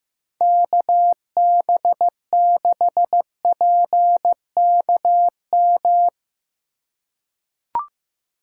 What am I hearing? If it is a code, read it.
Morse code, heard as a keyed tone, about 15 wpm, 700 Hz. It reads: KB6PKM
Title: USA Call Signs - 2x Character Spacing - 15wpm